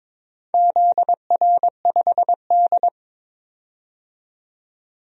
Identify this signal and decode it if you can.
Morse code, heard as a keyed tone, about 22 wpm, 700 Hz. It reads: ZR5D
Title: DXCC-100 Call Signs - 4 Characters - 22wpm